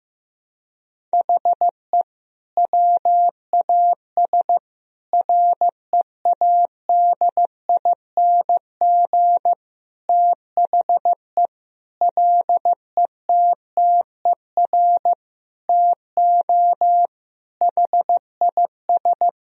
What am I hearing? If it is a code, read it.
Morse code, heard as a keyed tone, about 15 wpm, 700 Hz. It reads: HE WAS READING THE LETTER TO HIS
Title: Sentences from Top 1000 Words - Review Part 3 - 15wpm